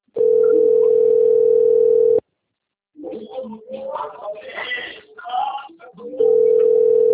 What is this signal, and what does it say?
Ringback tone (North America), 440 + 480 Hz, with speech.
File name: rg-1600-unknown- - - .4635